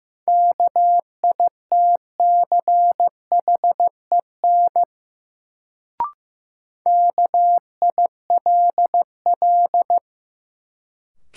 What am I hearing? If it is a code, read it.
Morse code, heard as a keyed tone, about 15 wpm, 700 Hz. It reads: KITCHEN KILL
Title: Sentences from Top 1400 Words - New Word Drill Alt - 15wpm